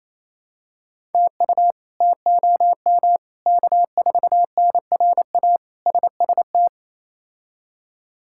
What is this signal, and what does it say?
Morse code, heard as a keyed tone, about 28 wpm, 700 Hz. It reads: TU TOM K4NRA SST
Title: K1USN SST QSO Practice - 28wpm